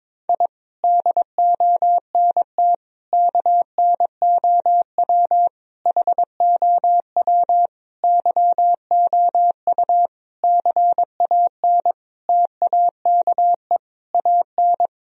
Morse code, heard as a keyed tone, about 22 wpm, 700 Hz. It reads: I DONT KNOW HOW YOU CAN TAKE AN
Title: Sentences from Top 1000 Words - Review Part 2 - 22wpm